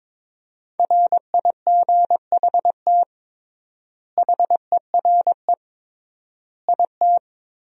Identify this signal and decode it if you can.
Morse code, heard as a keyed tone, about 22 wpm, 700 Hz. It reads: RIGHT HERE IT
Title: 3x Word-Spacing - Sentences from Top 300 Words - 22wpm